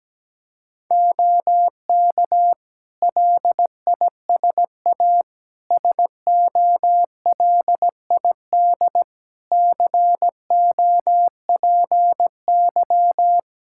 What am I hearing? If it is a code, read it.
Morse code, heard as a keyed tone, about 17 wpm, 700 Hz. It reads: OK LISA SOLID COPY
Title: Practice QSO - Exchange 3 - More Info - 17wpm